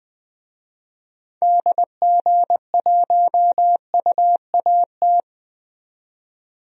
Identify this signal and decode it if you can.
Morse code, heard as a keyed tone, about 20 wpm, 700 Hz. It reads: DG1UAT
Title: DXCC-100 Call Signs - 6 Characters - 20wpm